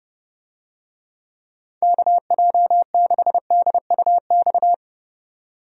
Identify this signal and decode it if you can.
Morse code, heard as a keyed tone, about 31 wpm, 700 Hz. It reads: KJ6DUX
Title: USA Call Signs - 6 Characters - 30wpm